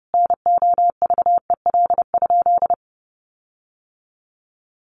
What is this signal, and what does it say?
Morse code, heard as a keyed tone, about 30 wpm, 700 Hz. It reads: NOVEL?
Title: Sentences from Top 1300 Words 30wpm